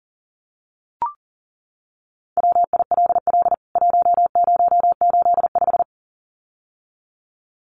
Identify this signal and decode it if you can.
Morse code, heard as a keyed tone, about 40 wpm, 700 Hz. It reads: WILL 1085
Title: CWT Exchange 40wpm